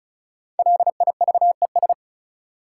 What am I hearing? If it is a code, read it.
Morse code, heard as a keyed tone, about 36 wpm, 700 Hz. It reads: LIVES
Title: Sets of 5 Words Encore - New Word Drill - 35wpm